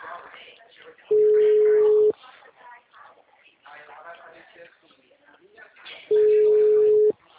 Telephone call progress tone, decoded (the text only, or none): ringback tone